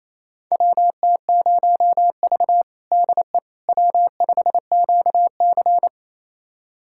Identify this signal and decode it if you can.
Morse code, heard as a keyed tone, about 28 wpm, 700 Hz. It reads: WT0V DE W5QC
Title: USA Callsign DE Callsign - 4 Characters - 28wpm